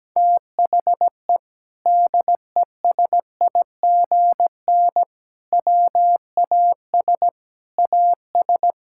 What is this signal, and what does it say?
Morse code, heard as a keyed tone, about 17 wpm, 700 Hz. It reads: THE DESIGN WAS AS
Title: Sentences from Top 1000 Words - Review Part 5 - 17wpm